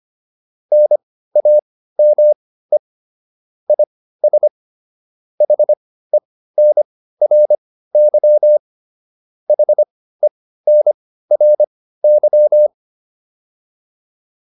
Morse code, heard as a keyed tone, about 25 wpm, spaced as for 15 wpm, 600 Hz. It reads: NAME IS HENRY HENRY
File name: QSO102 15